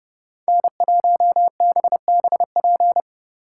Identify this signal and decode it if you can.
Morse code, heard as a keyed tone, about 35 wpm, 700 Hz. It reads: N1BBP